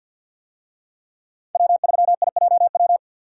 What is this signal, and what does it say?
Morse code, heard as a keyed tone, about 52 wpm, 700 Hz. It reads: W3IJW
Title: USA Call Signs - 5 Characters - 50wpm